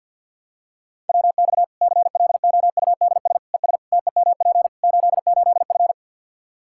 Morse code, heard as a keyed tone, about 46 wpm, 700 Hz. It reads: WX CLOUDS ES TEMP 88F
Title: K1USN SST QSO Practice 50wpm